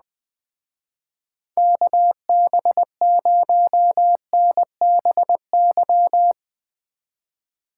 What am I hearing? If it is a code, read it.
Morse code, heard as a keyed tone, about 19 wpm, 700 Hz. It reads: KB0NBY
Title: Speed Racing - USA 2x3 Call Signs 17wpm